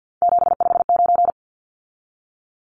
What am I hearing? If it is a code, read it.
Morse code, heard as a keyed tone, about 50 wpm, 700 Hz. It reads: 759
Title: Numbers - 3-Digit 50wpm